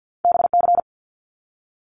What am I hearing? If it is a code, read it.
Morse code, heard as a keyed tone, about 50 wpm, 700 Hz. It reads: BC
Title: Canadian Provinces and Territories 50wpm